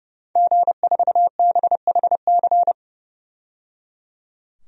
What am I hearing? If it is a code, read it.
Morse code, heard as a keyed tone, about 30 wpm, 700 Hz. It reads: G4BHC
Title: DXCC-100 Call Signs - 5 Characters - 30wpm